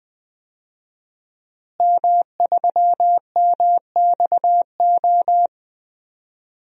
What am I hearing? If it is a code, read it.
Morse code, heard as a keyed tone, about 20 wpm, 700 Hz. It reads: M3MXO
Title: DXCC-100 Call Signs - 5 Characters - Repeated 2x - 20wpm